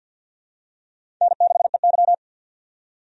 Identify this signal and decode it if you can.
Morse code, heard as a keyed tone, about 51 wpm, 700 Hz. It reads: N6EY
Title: USA Call Signs - 4 Characters - 50wpm